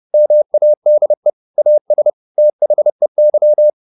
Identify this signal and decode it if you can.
Morse code, heard as a keyed tone, about 30 wpm, 600 Hz. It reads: MADE AS THEY
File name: the old man and the sea / the old man and the sea 1 30wpm